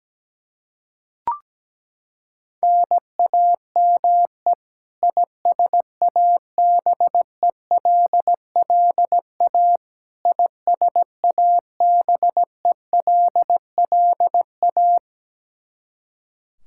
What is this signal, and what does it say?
Morse code, heard as a keyed tone, about 17 wpm, 700 Hz. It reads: NAME ISABELLA ISABELLA
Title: K1USN SST QSO Practice 17wpm